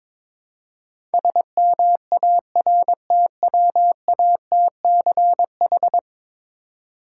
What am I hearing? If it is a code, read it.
Morse code, heard as a keyed tone, about 22 wpm, 700 Hz. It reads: SMARTWATCH
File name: Difficult Words with Hints - 22wpm